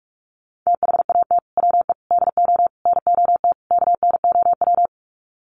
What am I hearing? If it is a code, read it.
Morse code, heard as a keyed tone, about 44 wpm, 700 Hz. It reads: THAT WE DO NOT KNOW
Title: Sets of 5 Words Encore2 45wpm